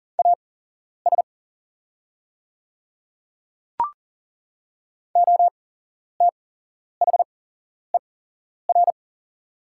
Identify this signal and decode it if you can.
Morse code, heard as a keyed tone, about 41 wpm, spaced as for 11 wpm, 690 Hz. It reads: AS OTHER
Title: 8x Character Spacing - Top 100 Words - 40wpm